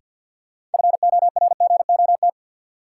Morse code, heard as a keyed tone, about 52 wpm, 700 Hz. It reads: FORGOT
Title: Sentences from Top 1400 Words - New Word Drill Alt - 50wpm